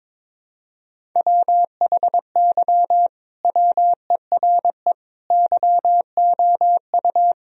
Morse code, heard as a keyed tone, about 22 wpm, 700 Hz. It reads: WHY WERE YOU
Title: Sentences from Top 1000 Words - Review Part 4 - 22wpm